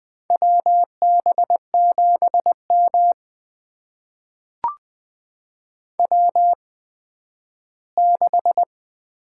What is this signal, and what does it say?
Morse code, heard as a keyed tone, about 20 wpm, spaced as for 12 wpm, 700 Hz. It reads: WB7M W6
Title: USA Call Signs - 4 Characters - 8x Character Spacing - 20wpm